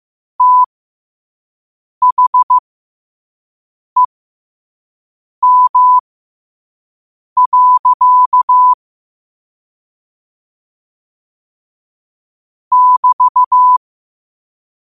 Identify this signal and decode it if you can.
Morse code, heard as a keyed tone, about 15 wpm, spaced as for 5 wpm, 1000 Hz. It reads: THEM. <BT>
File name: qotdfiles / 5wpm